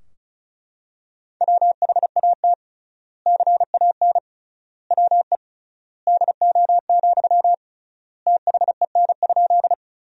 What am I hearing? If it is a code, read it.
Morse code, heard as a keyed tone, about 34 wpm, 700 Hz. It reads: WHAT CAN WE DO, THEN?
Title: 3x Word-Spacing - Sentences from Top 100 Words - 35wpm